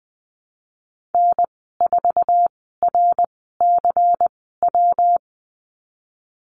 Morse code, heard as a keyed tone, about 20 wpm, spaced as for 15 wpm, 700 Hz. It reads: N4RCW
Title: 167 Lesson 27g - Number 7 - Call Signs 20wpm